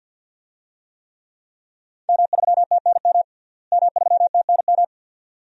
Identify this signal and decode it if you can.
Morse code, heard as a keyed tone, about 50 wpm, 700 Hz. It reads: M3TNM M3TNM
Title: DXCC-100 Call Signs - 5 Characters - Repeated 2x - 50wpm